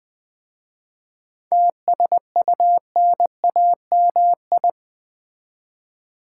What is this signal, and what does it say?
Morse code, heard as a keyed tone, about 20 wpm, 700 Hz. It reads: TSUNAMI
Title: 7 Letter Words - First 3 Letters Given - 20wpm